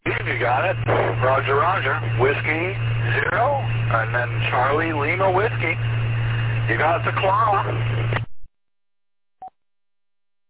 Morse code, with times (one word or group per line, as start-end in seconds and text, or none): none